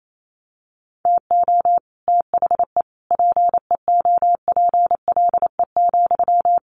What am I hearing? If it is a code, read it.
Morse code, heard as a keyed tone, about 28 wpm, 700 Hz. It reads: TO THE PEOPLE,
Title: Sentences from Top 1500 Words 28wpm